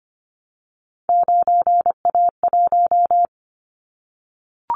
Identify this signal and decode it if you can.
Morse code, heard as a keyed tone, about 25 wpm, 700 Hz. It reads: 9A1
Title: Callsign Prefixes for Top 25 DXCC - Repeated 3x - 25wpm